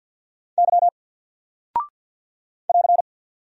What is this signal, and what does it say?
Morse code, heard as a keyed tone, about 48 wpm, 700 Hz. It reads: Y <AR>
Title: Single Character (Letter:Number:Punctuation:Prosign) Alt 50wpm